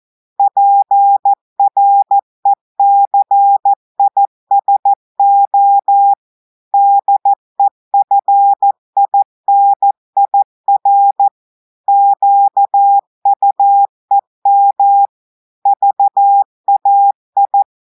Morse code, heard as a keyed tone, about 14 wpm, 800 Hz. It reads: PRECISO DEFINIR QUEM VAI
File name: audios / portugais / portugais texte en clair 2 vitesse 14 mots minutes